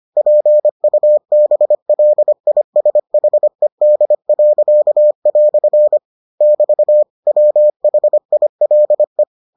Morse code, heard as a keyed tone, about 25 wpm, 600 Hz. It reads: PUBLISHED." <BT> WHILE